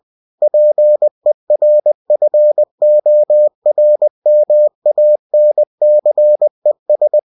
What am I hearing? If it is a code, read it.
Morse code, heard as a keyed tone, about 20 wpm, 600 Hz. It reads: PERFORMANCES
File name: cw-books / winnie-the-pooh / winnie-20wpm-page-0112